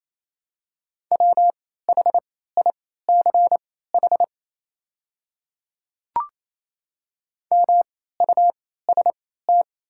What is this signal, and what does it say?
Morse code, heard as a keyed tone, about 27 wpm, spaced as for 16 wpm, 700 Hz. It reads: WHICH MUST
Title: 3x Character Spacing - Top 300 Words - 28wpm